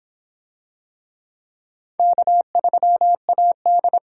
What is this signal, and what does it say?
Morse code, heard as a keyed tone, about 24 wpm, 700 Hz. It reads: K3AD